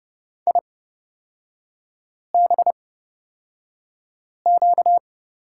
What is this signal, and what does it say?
Morse code, heard as a keyed tone, about 30 wpm, spaced as for 5 wpm, 690 Hz. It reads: IBQ
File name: course / 30:10wpm / 236 Lesson 38e - Letter q - Three Characters 30wpm